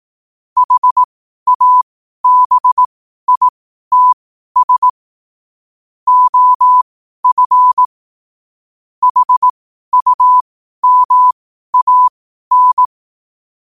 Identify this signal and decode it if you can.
Morse code, heard as a keyed tone, about 18 wpm, spaced as for 12 wpm, 1000 Hz. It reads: HABITS OF HUMAN
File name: qotdfiles / 12wpm